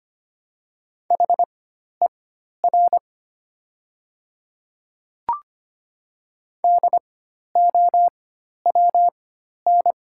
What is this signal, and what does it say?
Morse code, heard as a keyed tone, about 25 wpm, spaced as for 12 wpm, 700 Hz. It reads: HER DOWN
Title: 4x Character Spacing - Top 200 Words - 25wpm